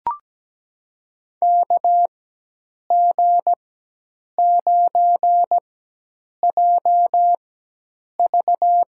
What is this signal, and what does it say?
Morse code, heard as a keyed tone, about 17 wpm, spaced as for 8 wpm, 700 Hz. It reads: KG9JV
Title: USA Call Signs - 5 Characters - 4x Character Spacing - 17wpm